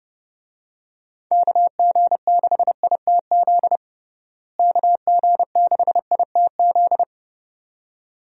Morse code, heard as a keyed tone, about 30 wpm, 700 Hz. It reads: KG6ITZ KG6ITZ
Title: USA Call Signs - 6 Characters - Repeated 2x - 30wpm